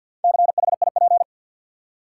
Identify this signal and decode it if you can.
Morse code, heard as a keyed tone, about 52 wpm, 700 Hz. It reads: CHIP
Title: Four Letter Words - 50wpm